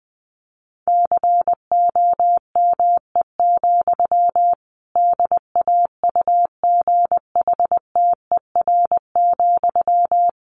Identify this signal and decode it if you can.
Morse code, heard as a keyed tone, about 20 wpm, 700 Hz. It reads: COME, DAUGHTER,
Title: Sentences from Top 1500 Words 20wpm